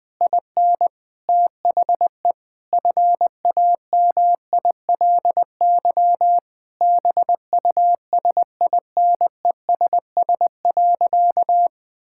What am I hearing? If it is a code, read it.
Morse code, heard as a keyed tone, about 20 wpm, 700 Hz. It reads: IN THE FAMILY BUSINESS.
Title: Sentences from Top 1000 Words - Review Part 5 - 20wpm